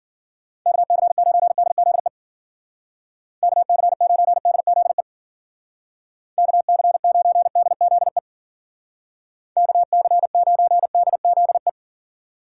Morse code, heard as a keyed tone, about 52 wpm, 700 Hz. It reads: KC9DZE KC9DZE KC9DZE KC9DZE